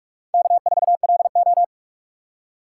Morse code, heard as a keyed tone, about 46 wpm, 700 Hz. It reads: K4LO
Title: USA Call Signs - 4 Characters - 4x Character Spacing - 45wpm